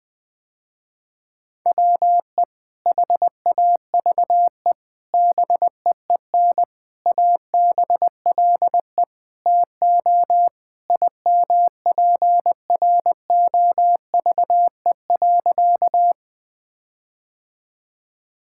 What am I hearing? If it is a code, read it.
Morse code, heard as a keyed tone, about 20 wpm, 700 Hz. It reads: WE HAVE BEEN ABLE TO IMPROVE.
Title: Sentences with 4-Word Phrases - Spoken Beginning - 20wpm